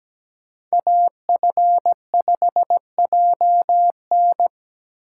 Morse code, heard as a keyed tone, about 17 wpm, 700 Hz. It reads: AF5JN
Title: USA Call Signs - 4x Character Spacing - 17wpm